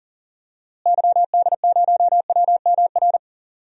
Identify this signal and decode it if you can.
Morse code, heard as a keyed tone, about 40 wpm, 700 Hz. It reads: YD0WMR
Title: DXCC-100 Call Signs - 6 Characters - Repeated 3x - 40wpm